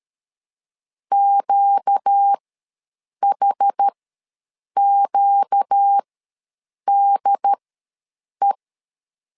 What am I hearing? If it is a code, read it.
Morse code, heard as a keyed tone, about 13 wpm, spaced as for 7 wpm, 800 Hz. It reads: QHQDE
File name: LEZIONI / 4 Q ws 12wpm 1912